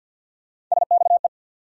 Morse code, heard as a keyed tone, about 54 wpm, 700 Hz. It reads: SXE
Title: USA Call Sign Postfixes - 3 Letters 50wpm